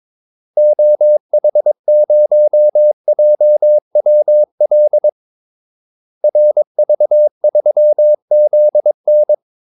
Morse code, heard as a keyed tone, about 22 wpm, 600 Hz. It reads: OH0JWL RV3ZN
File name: Callsigns-22wpm